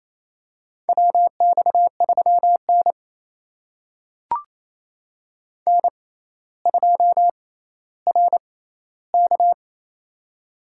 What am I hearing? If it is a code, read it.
Morse code, heard as a keyed tone, about 28 wpm, spaced as for 14 wpm, 700 Hz. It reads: WX3N N2RK
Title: USA Call Signs - 4 Characters - 6x Character Spacing - 28wpm